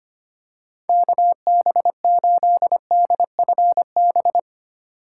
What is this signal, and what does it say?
Morse code, heard as a keyed tone, about 25 wpm, 700 Hz. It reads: KB8DFB
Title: USA Call Signs - 8x Character Spacing - 25wpm